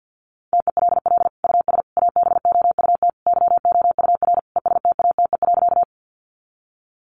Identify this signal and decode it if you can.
Morse code, heard as a keyed tone, about 50 wpm, 700 Hz. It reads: TELL US ABOUT YOUR ESTATE.
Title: Sentences from Top 1200 Words 50wpm